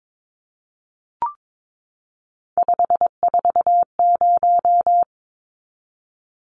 Morse code, heard as a keyed tone, about 22 wpm, 700 Hz. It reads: 540
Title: Numbers - 3-Digit 22wpm